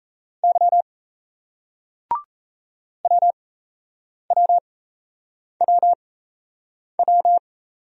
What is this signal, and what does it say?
Morse code, heard as a keyed tone, about 37 wpm, spaced as for 8 wpm, 700 Hz. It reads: Y WWWW